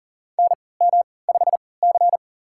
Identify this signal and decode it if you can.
Morse code, heard as a keyed tone, about 40 wpm, spaced as for 23 wpm, 700 Hz. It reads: NM5C
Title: USA Call Signs - 4 Characters - 3x Character Spacing - 40wpm